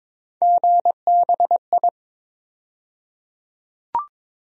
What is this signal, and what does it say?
Morse code, heard as a keyed tone, about 21 wpm, 700 Hz. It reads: GBI
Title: USA Call Sign Postfixes - 3 Letters 22wpm